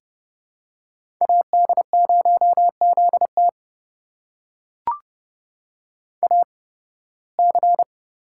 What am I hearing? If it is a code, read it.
Morse code, heard as a keyed tone, about 29 wpm, spaced as for 19 wpm, 700 Hz. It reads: AD0ZT AC
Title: USA Call Signs - 5 Characters - 8x Character Spacing - 30wpm